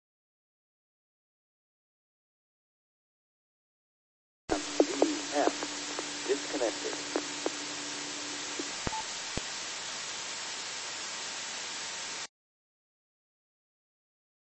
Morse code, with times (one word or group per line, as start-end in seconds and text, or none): none